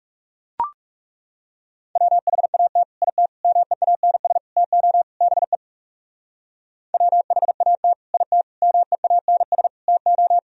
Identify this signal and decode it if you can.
Morse code, heard as a keyed tone, about 38 wpm, 700 Hz. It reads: WHAT IT MEANS TO BE WHAT IT MEANS TO